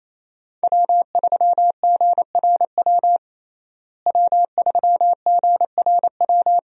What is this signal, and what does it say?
Morse code, heard as a keyed tone, about 27 wpm, 700 Hz. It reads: W3GRW W3GRW
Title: USA Call Signs - 5 Characters - Repeated 2x - 28wpm